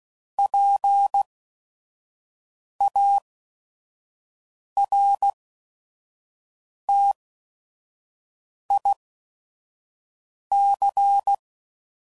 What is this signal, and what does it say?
Morse code, heard as a keyed tone, about 16 wpm, 800 Hz. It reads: P A R T I C